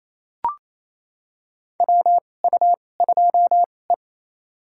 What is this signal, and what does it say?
Morse code, heard as a keyed tone, about 28 wpm, spaced as for 20 wpm, 700 Hz. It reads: WU2E
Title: USA Call Signs - 4 Characters - 2x Character Spacing - 28wpm